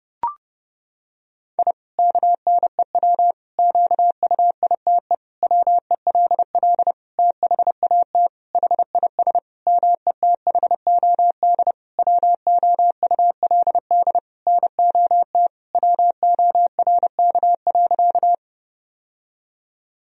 Morse code, heard as a keyed tone, about 32 wpm, 700 Hz. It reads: I KNEW QUITE WELL THAT HIS METHOD WOULD NOT WORK.
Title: Sentences from Top 1000 Words - Review Part 3 - 30wpm